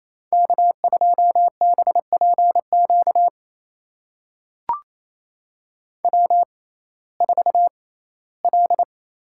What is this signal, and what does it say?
Morse code, heard as a keyed tone, about 28 wpm, spaced as for 17 wpm, 700 Hz. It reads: K2BPQ W4L
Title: USA Call Signs - 5 Characters - 6x Character Spacing - 28wpm